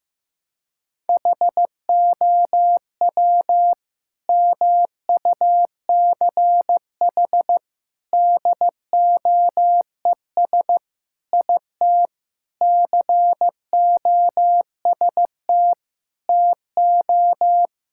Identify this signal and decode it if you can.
Morse code, heard as a keyed tone, about 15 wpm, 700 Hz. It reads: HOW MUCH DOES IT COST TO
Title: Sentences from Top 1600 Words 15wpm